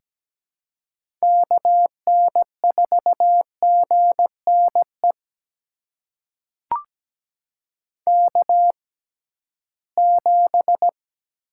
Morse code, heard as a keyed tone, about 17 wpm, spaced as for 13 wpm, 700 Hz. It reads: KN4GNE K7
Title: USA Call Signs - 6x Character Spacing - 17wpm